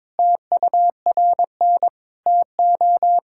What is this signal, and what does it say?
Morse code, heard as a keyed tone, about 22 wpm, 700 Hz. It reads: TURN TO
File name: practice / 22wpm / Sentences with 3-Word Phrases - 22wpm